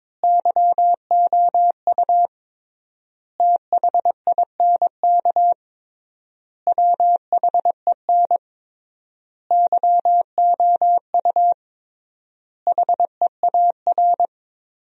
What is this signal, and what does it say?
Morse code, heard as a keyed tone, about 21 wpm, 700 Hz. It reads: YOU THINK WHEN YOU HEAR
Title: 3x Word-Spacing - Sentences from Top 300 Words - 22wpm